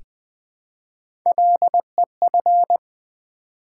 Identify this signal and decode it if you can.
Morse code, heard as a keyed tone, about 20 wpm, 700 Hz. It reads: LEF
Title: Build A Word - 1 Letter At A Time - 4-Letter Words - 20wpm